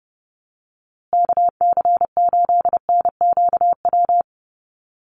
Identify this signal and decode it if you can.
Morse code, heard as a keyed tone, about 30 wpm, 700 Hz. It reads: KC8NQW
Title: USA Call Signs Repeated 3x 30wpm